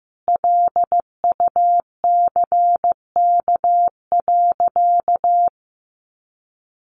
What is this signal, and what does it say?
Morse code, heard as a keyed tone, about 15 wpm, 700 Hz. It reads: LUCK.
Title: Sentences from Top 1500 Words 15wpm